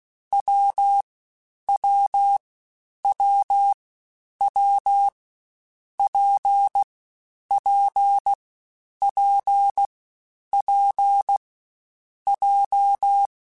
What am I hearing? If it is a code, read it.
Morse code, heard as a keyed tone, about 16 wpm, 800 Hz. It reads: W W W W P P P P J
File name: Difficult Combos / Difficult Combos-mp3s / WPJ WPJ1 @15